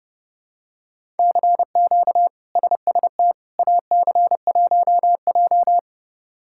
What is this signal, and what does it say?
Morse code, heard as a keyed tone, about 32 wpm, 700 Hz. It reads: CQ SST AC1J
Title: K1USN SST QSO Practice - 30wpm